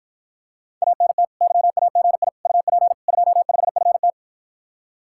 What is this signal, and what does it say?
Morse code, heard as a keyed tone, about 53 wpm, 700 Hz. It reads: ANT YAGI UP 25FT